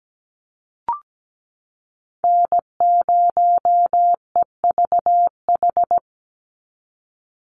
Morse code, heard as a keyed tone, about 17 wpm, 700 Hz. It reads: N0EVH
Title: NA SOTA Chaser Callsigns 17wpm